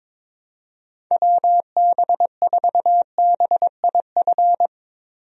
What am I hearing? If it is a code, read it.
Morse code, heard as a keyed tone, about 22 wpm, 700 Hz. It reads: WB4BIF
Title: USA Call Signs - 6 Characters - 22wpm